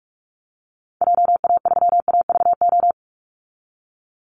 Morse code, heard as a keyed tone, about 44 wpm, 700 Hz. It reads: JA3AVO
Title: Call Signs - 45wpm